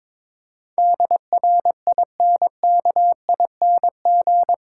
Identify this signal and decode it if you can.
Morse code, heard as a keyed tone, about 22 wpm, 700 Hz. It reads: DRINKING
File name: Difficult Sentences with Hints - 22wpm